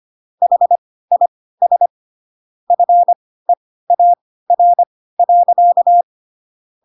Morse code, heard as a keyed tone, about 25 wpm, spaced as for 16 wpm, 700 Hz. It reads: HIS FEAR.